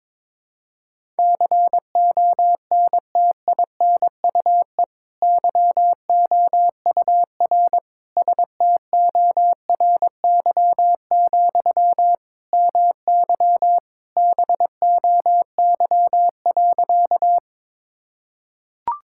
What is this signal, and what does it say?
Morse code, heard as a keyed tone, about 22 wpm, 700 Hz. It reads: CONTINUE YOUR STORY, MY BOY.
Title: Sentences from Top 1000 Words - Review Part 6 - 22wpm